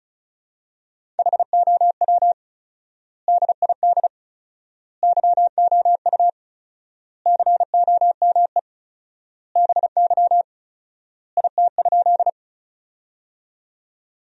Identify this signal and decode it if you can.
Morse code, heard as a keyed tone, about 34 wpm, 700 Hz. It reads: HOW DID YOU COME BY IT?
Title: 4x Word-Spacing - Sentences from Top 100 Words - 35wpm